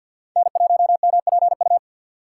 Morse code, heard as a keyed tone, about 53 wpm, 700 Hz. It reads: N1MPU
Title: USA Call Signs - 5 Characters - 8x Character Spacing - 50wpm